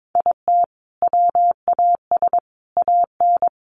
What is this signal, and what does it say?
Morse code, heard as a keyed tone, about 22 wpm, 700 Hz. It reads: IT WAS AN
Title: Sentences from Top 1500 Words 22wpm